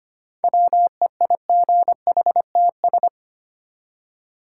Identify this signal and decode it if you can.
Morse code, heard as a keyed tone, about 25 wpm, 700 Hz. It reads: WEIGHTS
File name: practice / 25wpm / Easy Words with Hints - 25wpm